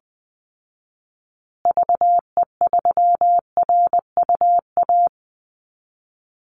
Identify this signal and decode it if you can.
Morse code, heard as a keyed tone, about 20 wpm, 700 Hz. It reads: VE3RUA
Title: Call Signs Repeated 2x - 20wpm